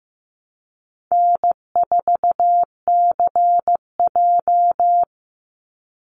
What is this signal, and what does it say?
Morse code, heard as a keyed tone, about 15 wpm, 700 Hz. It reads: N4CJ
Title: Easy Call Signs Repeated 2x - 15wpm